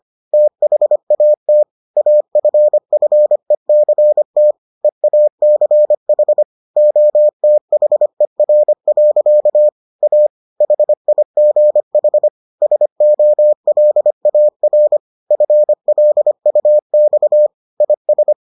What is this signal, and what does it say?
Morse code, heard as a keyed tone, about 25 wpm, 600 Hz. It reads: THAT AFFECT EACH OTHER. A HIGH SOLAR FLUX IS